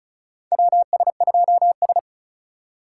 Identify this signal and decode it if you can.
Morse code, heard as a keyed tone, about 36 wpm, 700 Hz. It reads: WS2S
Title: USA Call Signs - 4 Characters - 3x Character Spacing - 35wpm